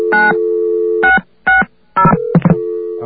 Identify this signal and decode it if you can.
DTMF, heard as telephone touch tones, tones of about 160 ms. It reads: 9660